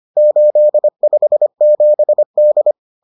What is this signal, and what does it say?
Morse code, heard as a keyed tone, about 25 wpm, 600 Hz. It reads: 857D